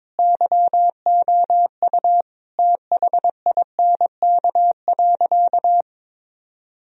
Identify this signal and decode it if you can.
Morse code, heard as a keyed tone, about 22 wpm, 700 Hz. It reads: YOU THINK.
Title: Sentences from Top 1000 Words - Review Part 2 - 22wpm